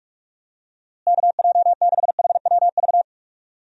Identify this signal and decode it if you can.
Morse code, heard as a keyed tone, about 44 wpm, 700 Hz. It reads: KJ6HWV
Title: USA Call Signs - 6 Characters - Repeated 2x - 45wpm